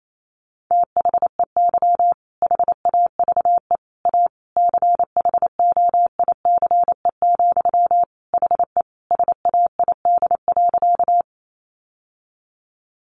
Morse code, heard as a keyed tone, about 28 wpm, 690 Hz. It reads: THEY HAVE A CHOICE, HE SAID.
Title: Sentences from Top 900 Words 28wpm